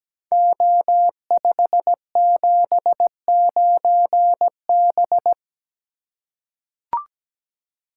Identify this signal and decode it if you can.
Morse code, heard as a keyed tone, about 17 wpm, 700 Hz. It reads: O579B
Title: Code Groups - 5 Letter:Numbers - 17wpm